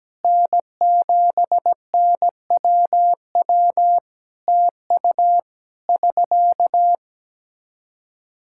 Morse code, heard as a keyed tone, about 17 wpm, 700 Hz. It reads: N7NWW TU <SK>
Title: Practice QSO - Exchange 4 - Ending - 17wpm